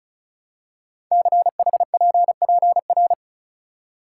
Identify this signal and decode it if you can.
Morse code, heard as a keyed tone, about 36 wpm, 700 Hz. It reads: CHPPR
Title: Code Groups - 5 Letters - 35wpm